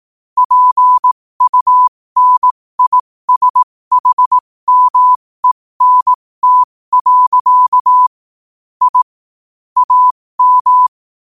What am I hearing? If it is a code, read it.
Morse code, heard as a keyed tone, about 18 wpm, spaced as for 15 wpm, 1000 Hz. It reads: PUNISHMENT. I AM
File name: qotdfiles / 15wpm